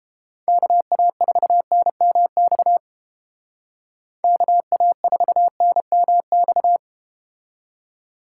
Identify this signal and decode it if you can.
Morse code, heard as a keyed tone, about 32 wpm, 700 Hz. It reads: KA4NMX KA4NMX